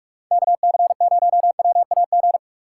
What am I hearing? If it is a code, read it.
Morse code, heard as a keyed tone, about 46 wpm, 700 Hz. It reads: KC0WAG
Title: USA Call Signs - 6 Characters - 3x Character Spacing - 45wpm